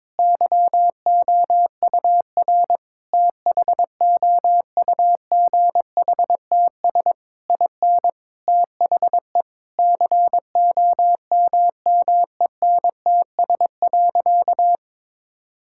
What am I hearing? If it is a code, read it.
Morse code, heard as a keyed tone, about 22 wpm, 700 Hz. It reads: YOUR THOUGHTS IN THE COMMENTS.
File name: practice / 22wpm / Sentences with 8-Word Phrases - 22wpm